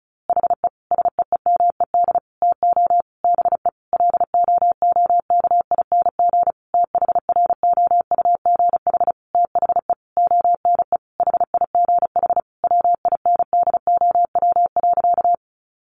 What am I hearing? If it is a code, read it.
Morse code, heard as a keyed tone, about 34 wpm, 700 Hz. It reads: HE SEEMED TO BE LOOKING THROUGH THE ONE HIGH WINDOW.
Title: A Sound Of Thunder 35wpm